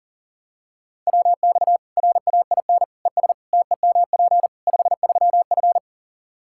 Morse code, heard as a keyed tone, about 40 wpm, 700 Hz. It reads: WX RAIN ES TEMP 53F
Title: Practice QSO - Exchange 2 - Info - 40wpm